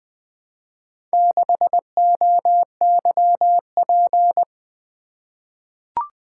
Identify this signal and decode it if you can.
Morse code, heard as a keyed tone, about 21 wpm, 700 Hz. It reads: 6OYP
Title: Four Letter-Number Combinations - 20wpm